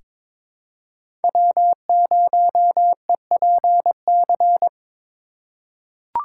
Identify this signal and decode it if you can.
Morse code, heard as a keyed tone, about 22 wpm, 700 Hz. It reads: W0EPC
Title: K1USN SST QSO Practice - 22wpm